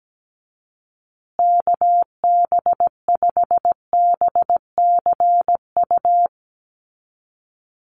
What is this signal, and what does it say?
Morse code, heard as a keyed tone, about 17 wpm, 700 Hz. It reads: KB5BCU
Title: USA Call Signs Repeated 2x 17wpm